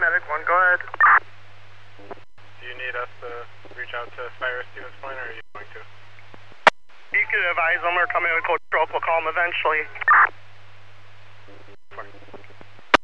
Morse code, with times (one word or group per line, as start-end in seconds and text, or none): none